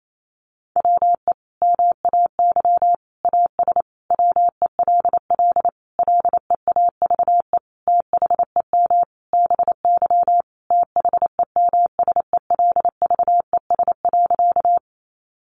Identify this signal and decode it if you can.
Morse code, heard as a keyed tone, about 28 wpm, 700 Hz. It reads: WE MAY AS WELL LEAVE THEM BY THEMSELVES.
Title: Sentences from Top 800 Words 28wpm